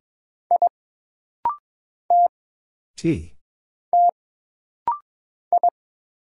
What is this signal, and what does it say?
Morse code, heard as a keyed tone, about 22 wpm, 700 Hz. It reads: I T T I
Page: This rapid-fire timing is geared to push your ICR (Instant Character Recognition) skills.